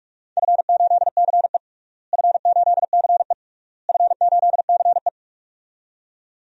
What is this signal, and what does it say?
Morse code, heard as a keyed tone, about 41 wpm, 700 Hz. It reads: F8CE F8CE F8CE
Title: DXCC-100 Call Signs - 4 Characters - Repeated 3x - 45wpm